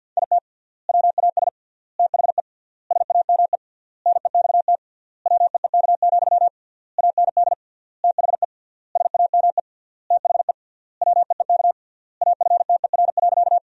Morse code, heard as a keyed tone, about 52 wpm, 700 Hz. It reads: IT WAS THE SAME NEXT WEEK, AND THE SAME THE WEEK AFTER.
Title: 3x Word-Spacing - Sentences from Top 500 Words - 50wpm